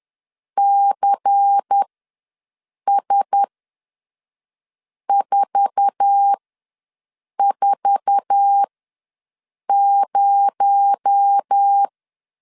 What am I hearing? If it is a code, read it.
Morse code, heard as a keyed tone, about 10 wpm, spaced as for 5 wpm, 800 Hz. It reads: CS440
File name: LEZIONI / 8 YJ45 / 8 YJ4 ws 10wpm 4020